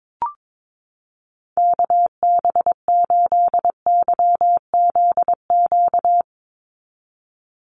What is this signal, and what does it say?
Morse code, heard as a keyed tone, about 22 wpm, 700 Hz. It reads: KB8YZQ
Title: USA Call Signs 22wpm